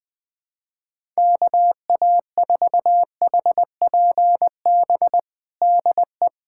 Morse code, heard as a keyed tone, about 20 wpm, 700 Hz. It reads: KA4HPB DE
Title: USA Callsign DE Callsign 20wpm